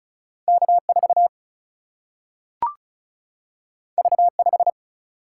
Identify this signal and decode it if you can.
Morse code, heard as a keyed tone, about 36 wpm, 700 Hz. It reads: K4 V5
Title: Single-Letter Single-Number 35wpm